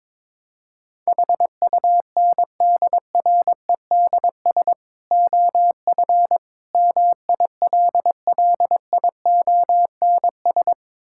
Morse code, heard as a keyed tone, about 22 wpm, 700 Hz. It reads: HUNDREDS OF MILLIONS
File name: practice / 22wpm / Build A Phrase - 5-Word Phrases - 22wpm